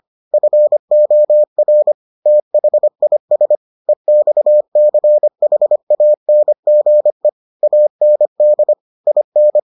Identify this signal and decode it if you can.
Morse code, heard as a keyed tone, about 25 wpm, 600 Hz. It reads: FOR THIS EXCHANGE AND IN